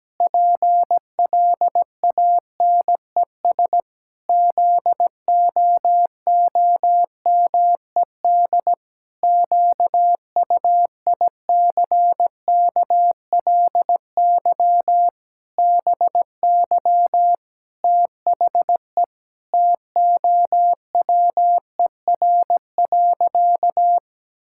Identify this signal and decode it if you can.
Morse code, heard as a keyed tone, about 17 wpm, 700 Hz. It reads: PLANES ZOOMED QUICKLY BY THE TOWER.
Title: Pangram Challenge v2 - 17wpm